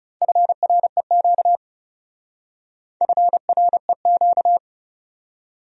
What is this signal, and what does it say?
Morse code, heard as a keyed tone, about 38 wpm, 700 Hz. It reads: FREQ FREQ